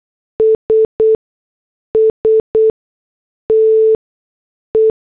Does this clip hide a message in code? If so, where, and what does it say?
Yes: Morse code from 0.4 to 4.9 s, heard as a keyed tone, about 8 wpm, spaced as for 6 wpm, 440 Hz. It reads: SSTE